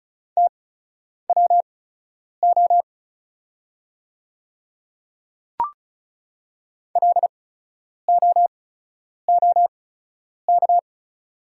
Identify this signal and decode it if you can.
Morse code, heard as a keyed tone, about 36 wpm, spaced as for 10 wpm, 700 Hz. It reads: TWO LOOK
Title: 8x Character Spacing - Top 100 Words - 35wpm